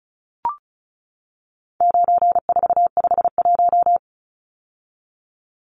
Morse code, heard as a keyed tone, about 34 wpm, 700 Hz. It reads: 9451
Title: Numbers - 4-Digit 35wpm